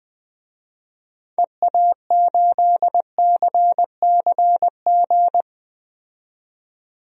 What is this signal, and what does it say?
Morse code, heard as a keyed tone, about 20 wpm, 700 Hz. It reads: EA8CCG
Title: DXCC-100 Call Signs - 6 Characters - 20wpm